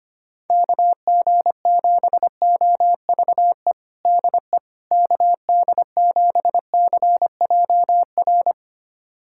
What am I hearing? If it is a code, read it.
Morse code, heard as a keyed tone, about 25 wpm, 700 Hz. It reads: KG7OVE DE KD7CJR
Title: USA Callsign DE Callsign 25wpm